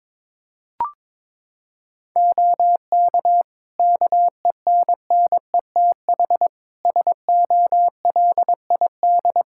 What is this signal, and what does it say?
Morse code, heard as a keyed tone, about 22 wpm, 700 Hz. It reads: OK KENNETH SOLID
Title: Practice QSO - Exchange 3 - More Info - 22wpm